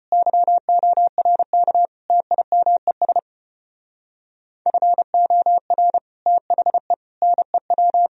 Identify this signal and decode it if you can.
Morse code, heard as a keyed tone, about 33 wpm, 700 Hz. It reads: YORK TIMES FOR THE NEW